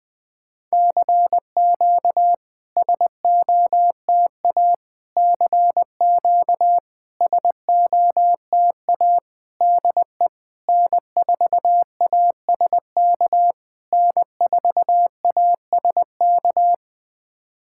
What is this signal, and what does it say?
Morse code, heard as a keyed tone, about 20 wpm, 700 Hz. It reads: CQ SOTA CQ SOTA DE N4ASK N4ASK
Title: SOTA QSO Practice 20wpm